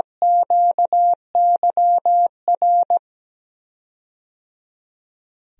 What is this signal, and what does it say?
Morse code, heard as a keyed tone, about 17 wpm, 700 Hz. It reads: QYR
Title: DXCC-100 Call Signs - 6 Characters - Repeated 3x - 17wpm